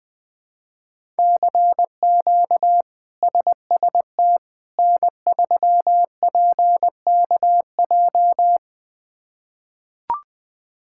Morse code, heard as a keyed tone, about 20 wpm, 700 Hz. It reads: CQ SST N3PKJ
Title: K1USN SST QSO Practice - 20wpm